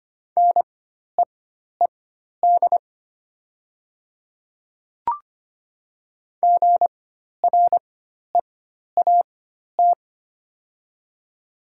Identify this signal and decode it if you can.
Morse code, heard as a keyed tone, about 25 wpm, spaced as for 12 wpm, 700 Hz. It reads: NEED GREAT
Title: 4x Character Spacing - Top 300 Words - 25wpm